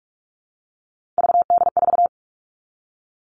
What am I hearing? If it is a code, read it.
Morse code, heard as a keyed tone, about 44 wpm, 700 Hz. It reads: VD4
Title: All Cut Numbers - 3-Digit 45wpm